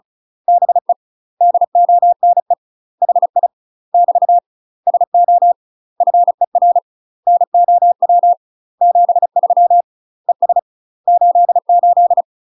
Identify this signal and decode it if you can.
Morse code, heard as a keyed tone, about 34 wpm, 700 Hz. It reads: BE DONE HI <BT> SO FER NOW 73 ES 88
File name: UK MASTER QSOs TRAINING / 35WPM / UK MASTER QSOs 0000